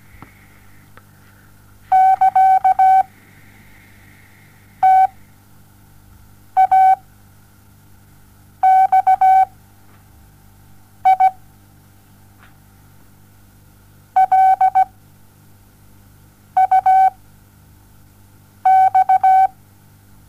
Morse code, heard as a keyed tone, about 17 wpm, spaced as for 5 wpm, 760 Hz. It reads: <KA>TAXI LUX
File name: Onlinelehrgang / morsen / DJ4UF Morsen07